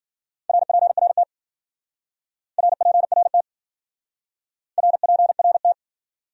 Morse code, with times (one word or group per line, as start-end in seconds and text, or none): 0.5-1.2 RPRT
2.6-3.4 RPRT
4.8-5.7 RPRT